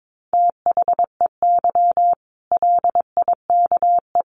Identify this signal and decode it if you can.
Morse code, heard as a keyed tone, about 22 wpm, 700 Hz. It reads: THEY LIKE
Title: Easy Phrases 22wpm